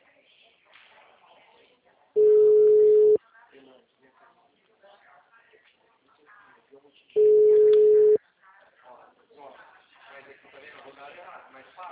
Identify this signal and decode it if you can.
Ringback tone (Europe, 425 Hz), with speech.